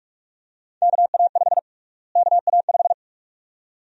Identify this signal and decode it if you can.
Morse code, heard as a keyed tone, about 47 wpm, 700 Hz. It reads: KA5 KA5
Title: USA Call Sign Prefixes - Repeated 2x - 45wpm